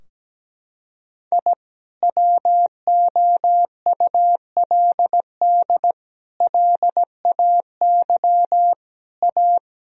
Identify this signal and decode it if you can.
Morse code, heard as a keyed tone, about 17 wpm, 700 Hz. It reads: I WOULD LAY A
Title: Sentences from Top 1000 Words - Review Part 3 - 17wpm